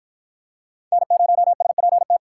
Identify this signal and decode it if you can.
Morse code, heard as a keyed tone, about 49 wpm, 700 Hz. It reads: N0SPT